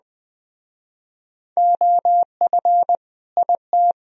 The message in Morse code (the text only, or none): OF IT